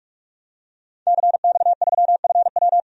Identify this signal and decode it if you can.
Morse code, heard as a keyed tone, about 46 wpm, 700 Hz. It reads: CX3FW
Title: DXCC-100 Call Signs - 5 Characters - 45wpm